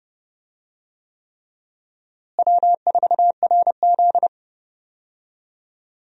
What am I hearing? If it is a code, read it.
Morse code, heard as a keyed tone, about 30 wpm, 700 Hz. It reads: W4RZ
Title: USA Call Signs - 4 Characters - Repeated 3x - 30wpm